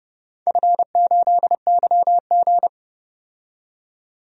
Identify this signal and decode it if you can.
Morse code, heard as a keyed tone, about 30 wpm, 700 Hz. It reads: F8YG
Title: DXCC-100 Call Signs - 4 Characters - 30wpm